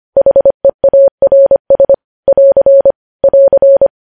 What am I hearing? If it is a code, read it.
Morse code, heard as a keyed tone, about 25 wpm, 550 Hz. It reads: HEARS <AR> <AR>
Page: news headlines in morse code at 25 wpm